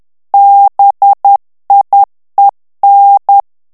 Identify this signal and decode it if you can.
Morse code, heard as a keyed tone, about 10 wpm, 800 Hz. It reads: BIEN
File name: FRANCAIS TEXTE CLAIR 2 VITESSE 10 MOTS MINUTES